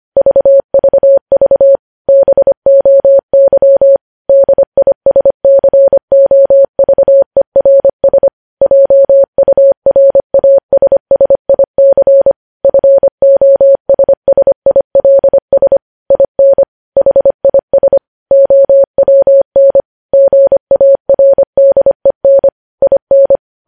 Morse code, heard as a keyed tone, about 25 wpm, 550 Hz. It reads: VVV BOY DISCOVERS JURASSIC FOSSILS IN HIS OWN GARDEN IN
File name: podcast / News headlines-1x-25x25